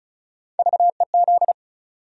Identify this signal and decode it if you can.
Morse code, heard as a keyed tone, about 36 wpm, 700 Hz. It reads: VEZ
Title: USA Call Sign Postfixes - 3 Letters 35wpm